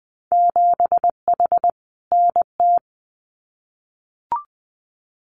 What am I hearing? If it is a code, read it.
Morse code, heard as a keyed tone, about 20 wpm, 700 Hz. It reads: 7H NT
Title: Winter Field Day Exchanges Encore 20wpm